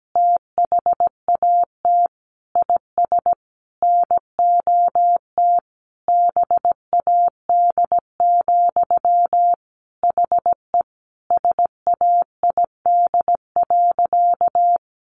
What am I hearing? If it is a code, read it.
Morse code, heard as a keyed tone, about 17 wpm, 700 Hz. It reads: THAT IS NOT BAD, HE SAID.
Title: Sentences from Top 700 Words 17wpm